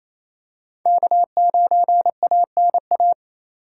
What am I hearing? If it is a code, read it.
Morse code, heard as a keyed tone, about 28 wpm, 700 Hz. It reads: K9ANA
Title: USA Call Signs - 5 Characters - 2x Character Spacing - 28wpm